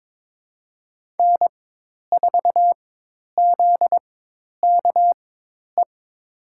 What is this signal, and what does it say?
Morse code, heard as a keyed tone, about 22 wpm, spaced as for 10 wpm, 700 Hz. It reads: N4ZKE
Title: USA Call Signs - 5 Characters - 4x Character Spacing - 22wpm